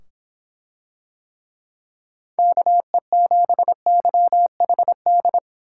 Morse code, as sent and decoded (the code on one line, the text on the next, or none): -.- . --... -.-- .... -..
KE7YHD